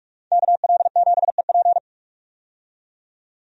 Morse code, heard as a keyed tone, about 41 wpm, 700 Hz. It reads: KL7EP
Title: DXCC-100 Call Signs - 5 Characters - 45wpm